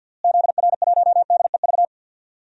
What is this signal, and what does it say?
Morse code, heard as a keyed tone, about 47 wpm, 700 Hz. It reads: ZR1DEV
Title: DXCC-100 Call Signs - 6 Characters - Repeated 3x - 50wpm